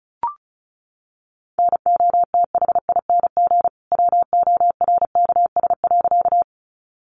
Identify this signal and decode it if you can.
Morse code, heard as a keyed tone, about 34 wpm, 700 Hz. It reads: NOTHING WORKS.
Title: Additional Easy Phrases 35wpm